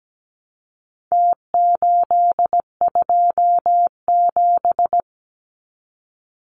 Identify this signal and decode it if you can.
Morse code, heard as a keyed tone, about 17 wpm, 700 Hz. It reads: T827
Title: Numbers - Contest Style - 4-Digit 17wpm